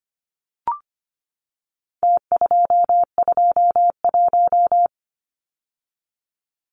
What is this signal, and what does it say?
Morse code, heard as a keyed tone, about 25 wpm, 700 Hz. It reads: T221
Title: Numbers - Contest Style - 4-Digit 25wpm